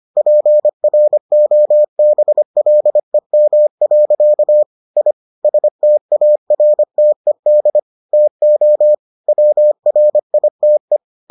Morse code, heard as a keyed tone, about 25 wpm, 600 Hz. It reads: PROBLEM. I STARTED TO WRITE